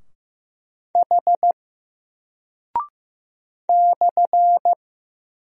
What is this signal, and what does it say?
Morse code, heard as a keyed tone, about 15 wpm, 700 Hz. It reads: H /
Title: Single Character (Letter:Number:Punctuation:Prosign) Alt 15wpm